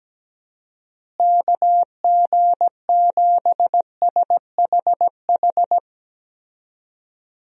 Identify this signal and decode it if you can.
Morse code, heard as a keyed tone, about 17 wpm, 700 Hz. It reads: KG7SHH
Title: USA Call Signs - 6 Characters - Repeated 3x - 17wpm